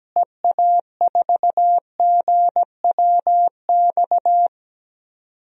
Morse code, heard as a keyed tone, about 17 wpm, 700 Hz. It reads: EA4GWX
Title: DXCC-100 Call Signs - 6 Characters - Repeated 2x - 17wpm